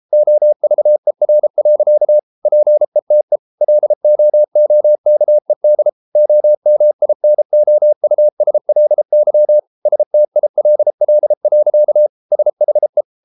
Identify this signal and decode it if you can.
Morse code, heard as a keyed tone, about 33 wpm, 600 Hz. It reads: OVER. PETE LOOKED OMINOUSLY STILL. SHE